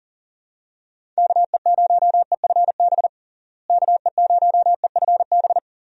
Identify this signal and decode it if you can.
Morse code, heard as a keyed tone, about 40 wpm, 700 Hz. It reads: KE0EFB KE0EFB
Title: USA Call Signs - 6 Characters - Repeated 3x - 40wpm